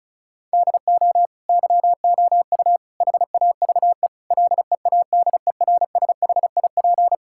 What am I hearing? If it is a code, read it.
Morse code, heard as a keyed tone, about 36 wpm, 700 Hz. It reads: DO YOU HAVE LEADERSHIP
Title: Sentences from Top 1600 Words 35wpm